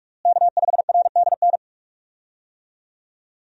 Morse code, heard as a keyed tone, about 46 wpm, 700 Hz. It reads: K5RDN
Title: USA Call Signs - 5 Characters - Repeated 2x - 45wpm